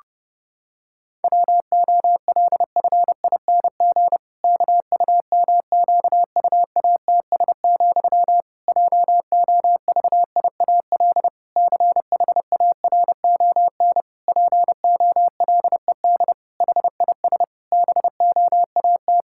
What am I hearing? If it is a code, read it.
Morse code, heard as a keyed tone, about 30 wpm, 700 Hz. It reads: WOLFING KUMQUATS, JOVIAL CHARON POLED HIS BOAT
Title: Pangram Challenge v2 - 30wpm